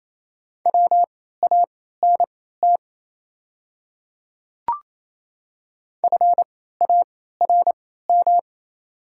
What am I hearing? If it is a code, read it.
Morse code, heard as a keyed tone, about 28 wpm, spaced as for 16 wpm, 700 Hz. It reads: WANT FARM
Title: 3x Character Spacing - Top 500 Words - 28wpm